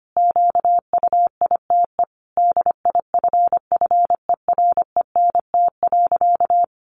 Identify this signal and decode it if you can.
Morse code, heard as a keyed tone, about 25 wpm, 700 Hz. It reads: QUITE DIFFERENT.
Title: Sentences from Top 700 Words 25wpm